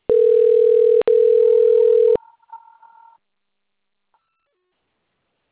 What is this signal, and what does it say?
Ringback tone (North America), 440 + 480 Hz.